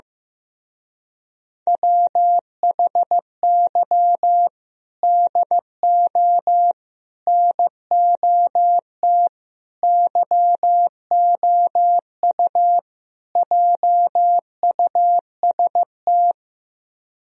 Morse code, heard as a keyed tone, about 15 wpm, 700 Hz. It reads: WHY DO NOT YOU JUST
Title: Speed-Racing - Sets of 5 Words 15wpm